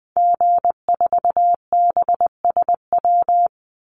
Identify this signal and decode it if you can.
Morse code, heard as a keyed tone, about 20 wpm, 700 Hz. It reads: G4BSW
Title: Call Signs Repeated 2x - 20wpm